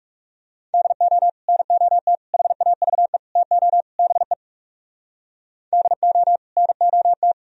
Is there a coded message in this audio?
Yes: Morse code from 0.7 to 7.3 s, heard as a keyed tone, about 38 wpm, 700 Hz. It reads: DO NOT HAVE TO BE DO NOT